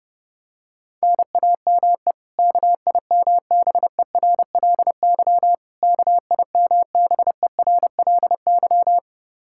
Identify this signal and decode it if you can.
Morse code, heard as a keyed tone, about 32 wpm, 700 Hz. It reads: NAME KIMBERLY KIMBERLY
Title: Practice QSO - Exchange 1 - Beginning - 30wpm